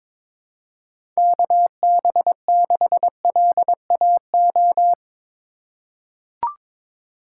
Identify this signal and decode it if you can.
Morse code, heard as a keyed tone, about 21 wpm, 700 Hz. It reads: KB6LAO
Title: USA Call Signs - 6 Characters - 2x Character Spacing - 22wpm